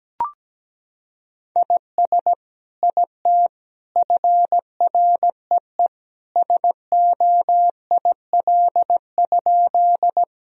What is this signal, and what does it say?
Morse code, heard as a keyed tone, about 17 wpm, 700 Hz. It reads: IS IT FREE SOIL?
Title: Sentences from Top 1000 Words - Review Part 4 - 17wpm